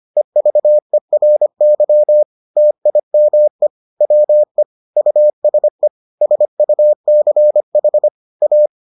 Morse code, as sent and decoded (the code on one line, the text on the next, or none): . ...- . .-. -.-- / - .. -- . / .-- . / ..- ... / ... ..- -.-. .... / .-
EVERY TIME WE USE SUCH A